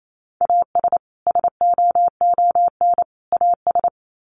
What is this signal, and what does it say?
Morse code, heard as a keyed tone, about 28 wpm, 700 Hz. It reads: AS SOON AS
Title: Sets of 3 Words 28wpm